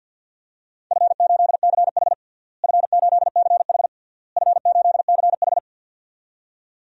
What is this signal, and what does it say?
Morse code, heard as a keyed tone, about 53 wpm, 700 Hz. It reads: F8CH F8CH F8CH
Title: DXCC-100 Call Signs - 4 Characters - Repeated 3x - 50wpm